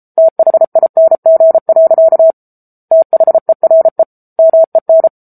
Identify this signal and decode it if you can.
Morse code, heard as a keyed tone, about 34 wpm, 650 Hz. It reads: THING. THERE MEN
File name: Book Two / 07-Chapter 7 - 33 WPM